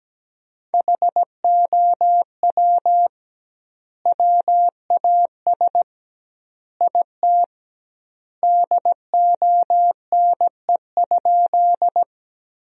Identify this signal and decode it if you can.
Morse code, heard as a keyed tone, about 17 wpm, 700 Hz. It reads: HOW WAS IT DONE?
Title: 2x Word-Spacing - Sentences from Top 500 Words - 17wpm